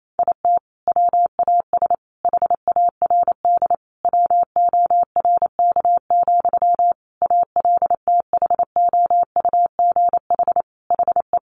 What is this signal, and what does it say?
Morse code, heard as a keyed tone, about 28 wpm, 700 Hz. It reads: IT WAS HARD WORK, ALTHOUGH HE
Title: Sentences from Top 600 Words 28wpm